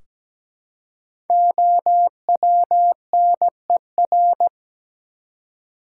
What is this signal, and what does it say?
Morse code, heard as a keyed tone, about 17 wpm, 700 Hz. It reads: OWNER
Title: Five Letter Words - 17wpm